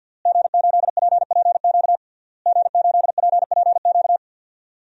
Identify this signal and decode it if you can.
Morse code, heard as a keyed tone, about 51 wpm, 700 Hz. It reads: G8PPQ G8PPQ
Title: DXCC-100 Call Signs - 5 Characters - Repeated 3x - 50wpm